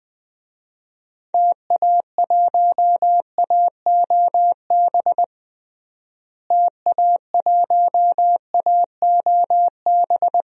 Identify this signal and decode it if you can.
Morse code, heard as a keyed tone, about 21 wpm, 700 Hz. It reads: TA1AOB TA1AOB
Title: DXCC-100 Call Signs - 6 Characters - Repeated 2x - 20wpm